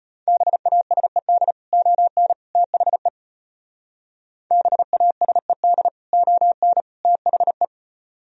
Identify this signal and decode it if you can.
Morse code, heard as a keyed tone, about 29 wpm, 700 Hz. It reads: BASED ON THE BASED ON THE
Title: Speed-Racing - Sets of 3 Words 25wpm